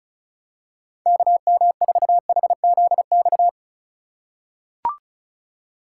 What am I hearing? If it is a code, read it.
Morse code, heard as a keyed tone, about 34 wpm, 700 Hz. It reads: KM4HZX
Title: USA Call Signs - 6 Characters - 3x Character Spacing - 35wpm